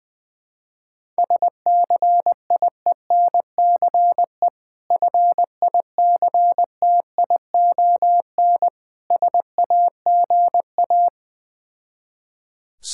Morse code, heard as a keyed tone, about 20 wpm, 690 Hz. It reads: SCIENCE FICTION SAGA
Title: Difficult Words with Hints - 20wpm